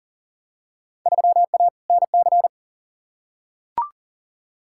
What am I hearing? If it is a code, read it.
Morse code, heard as a keyed tone, about 40 wpm, 700 Hz. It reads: 3A NC
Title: Field Day Exchanges - 40wpm